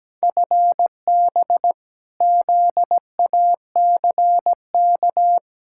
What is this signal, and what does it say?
Morse code, heard as a keyed tone, about 17 wpm, 700 Hz. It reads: FB ZACK
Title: Practice QSO - Exchange 3 - More Info - 17wpm